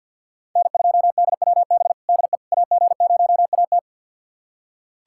Morse code, heard as a keyed tone, about 51 wpm, 690 Hz. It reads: N2DWB DE AG0AT
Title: USA Callsign DE Callsign - 5 Characters - 50wpm